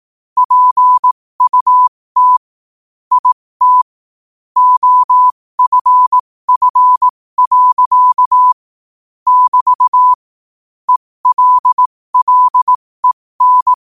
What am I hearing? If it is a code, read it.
Morse code, heard as a keyed tone, about 18 wpm, spaced as for 15 wpm, 1000 Hz. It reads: PUT IT OFF. <BT> ELLEN
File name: qotdfiles / 15wpm